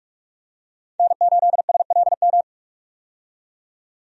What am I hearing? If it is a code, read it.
Morse code, heard as a keyed tone, about 45 wpm, 700 Hz. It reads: N8SLM
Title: USA Call Signs - 5 Characters - 45wpm